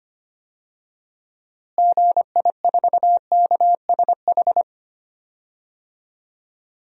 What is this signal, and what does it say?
Morse code, heard as a keyed tone, about 25 wpm, 700 Hz. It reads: GI4KSH
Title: DXCC-100 Call Signs - 6 Characters - Repeated 2x - 25wpm